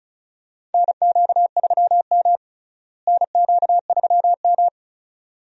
Morse code, heard as a keyed tone, about 36 wpm, 700 Hz. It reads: NQ3M NQ3M
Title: USA Call Signs - 4 Characters - Repeated 2x - 35wpm